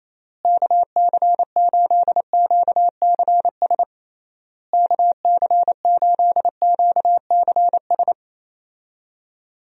Morse code, heard as a keyed tone, about 28 wpm, 700 Hz. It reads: KC8QCS KC8QCS
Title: USA Call Signs - 6 Characters - Repeated 3x - 28wpm